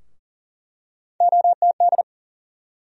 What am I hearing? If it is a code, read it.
Morse code, heard as a keyed tone, about 40 wpm, 700 Hz. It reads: OTD
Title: USA Call Sign Postfixes - 3 Letters 40wpm